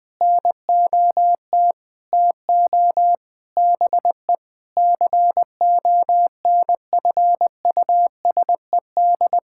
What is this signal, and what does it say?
Morse code, heard as a keyed tone, about 21 wpm, 700 Hz. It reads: NOT TO BE CONFUSED
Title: Sentences with 7-Word Phrases - Spoken Beginning - 20wpm